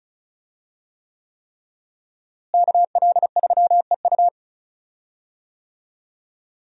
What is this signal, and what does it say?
Morse code, heard as a keyed tone, about 34 wpm, 700 Hz. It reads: KL3EU
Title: USA Call Signs - 5 Characters - Repeated 3x - 35wpm